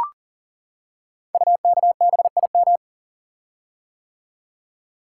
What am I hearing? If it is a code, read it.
Morse code, heard as a keyed tone, about 38 wpm, 700 Hz. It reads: UKBIM
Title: Code Groups - 5 Letters - 40wpm